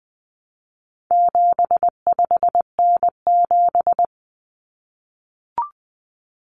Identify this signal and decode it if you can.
Morse code, heard as a keyed tone, about 20 wpm, 700 Hz. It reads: 75N7
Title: Common Cut Numbers - 4-Digit 20wpm